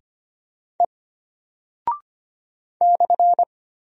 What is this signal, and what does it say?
Morse code, heard as a keyed tone, about 25 wpm, 700 Hz. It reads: E /
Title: Single Character (Letter:Number:Punctuation:Prosign) Alt 25wpm